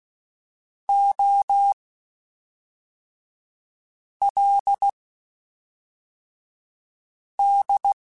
Morse code, heard as a keyed tone, about 16 wpm, spaced as for 3 wpm, 800 Hz. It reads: OLD
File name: (spaced) / QSOs Spaced-mp3s / WD0FBT 1 @5 sp